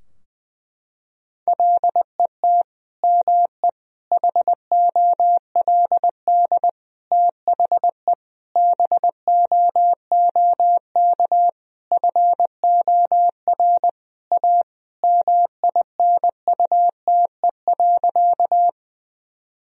Morse code, heard as a keyed tone, about 21 wpm, 700 Hz. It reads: LET ME HOLD THE BOOK FOR A MINUTE.
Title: Sentences from Top 1000 Words - Review Part 2 - 20wpm